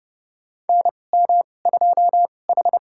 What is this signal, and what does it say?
Morse code, heard as a keyed tone, about 31 wpm, spaced as for 22 wpm, 700 Hz. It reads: NM2H
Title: USA Call Signs - 4 Characters - 2x Character Spacing - 30wpm